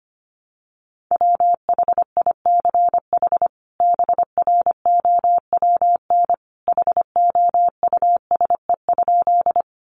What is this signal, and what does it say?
Morse code, heard as a keyed tone, about 25 wpm, 700 Hz. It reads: WHICH BROWN HOUSE?
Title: Sentences from Top 1200 Words 25wpm